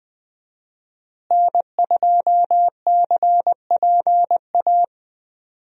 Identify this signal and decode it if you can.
Morse code, heard as a keyed tone, about 20 wpm, 700 Hz. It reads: N2CPA
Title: USA Call Signs - 5 Characters - Repeated 3x - 20wpm